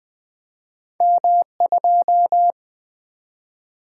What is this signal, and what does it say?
Morse code, heard as a keyed tone, about 20 wpm, 700 Hz. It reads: M2
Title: Two Letter-Number Combinations - 20wpm